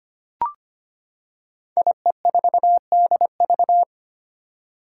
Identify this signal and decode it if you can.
Morse code, heard as a keyed tone, about 25 wpm, 700 Hz. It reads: IE4DV
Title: Five Challenging Letter-Number Combinations 25wpm